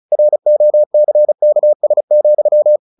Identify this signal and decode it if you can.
Morse code, heard as a keyed tone, about 34 wpm, 600 Hz. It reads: ROCKS,
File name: the old man and the sea / the old man and the sea 1 35wpm